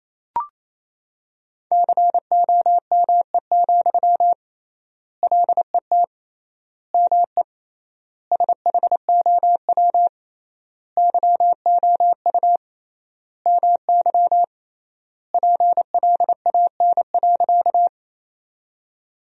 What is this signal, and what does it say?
Morse code, heard as a keyed tone, about 28 wpm, 700 Hz. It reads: COME, LET ME SHOW YOU MY PLAN.
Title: 3x Word-Spacing - Sentences from Top 500 Words - 28wpm